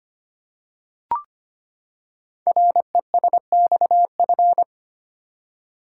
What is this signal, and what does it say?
Morse code, heard as a keyed tone, about 25 wpm, 700 Hz. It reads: RESXF
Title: Code Groups - 5 Letters - 25wpm